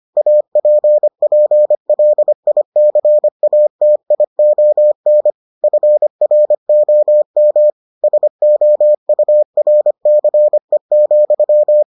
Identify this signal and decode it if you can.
Morse code, heard as a keyed tone, about 25 wpm, 600 Hz. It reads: APPLICATION FROM SOURCE,